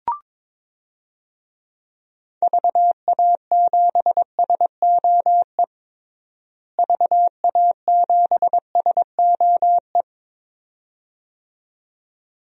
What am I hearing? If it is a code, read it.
Morse code, heard as a keyed tone, about 22 wpm, 700 Hz. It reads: VA7SOE VA7SOE
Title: DXCC-100 Call Signs - 6 Characters - Repeated 2x - 22wpm